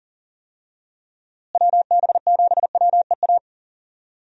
Morse code, heard as a keyed tone, about 40 wpm, 700 Hz. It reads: WB7WEA
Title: USA Call Signs - 6 Characters - Repeated 2x - 40wpm